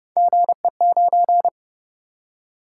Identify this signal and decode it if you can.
Morse code, heard as a keyed tone, about 29 wpm, 700 Hz. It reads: GE9